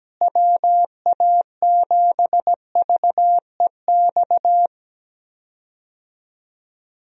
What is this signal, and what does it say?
Morse code, heard as a keyed tone, about 17 wpm, 700 Hz. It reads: WA7VEX
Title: USA Call Signs - 6 Characters - Repeated 2x - 17wpm